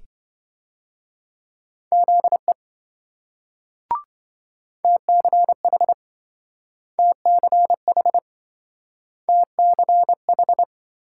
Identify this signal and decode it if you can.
Morse code, heard as a keyed tone, about 26 wpm, 700 Hz. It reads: ZE TCH TCH TCH